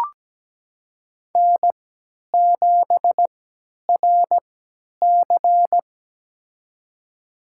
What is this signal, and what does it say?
Morse code, heard as a keyed tone, about 17 wpm, spaced as for 10 wpm, 700 Hz. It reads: N7RC
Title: USA Call Signs - 3x Character Spacing - 17wpm